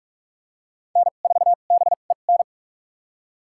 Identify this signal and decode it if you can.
Morse code, heard as a keyed tone, about 44 wpm, spaced as for 32 wpm, 700 Hz. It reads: N4BEN
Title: USA Call Signs - 5 Characters - 2x Character Spacing - 45wpm